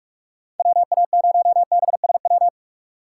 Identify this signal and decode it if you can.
Morse code, heard as a keyed tone, about 44 wpm, 690 Hz. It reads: WA0BSW
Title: USA Call Signs - 4x Character Spacing - 45wpm